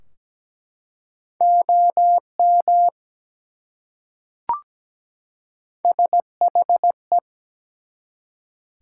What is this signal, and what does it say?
Morse code, heard as a keyed tone, about 17 wpm, 700 Hz. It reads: OM SHE
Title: Top 180 QSO Elements 17wpm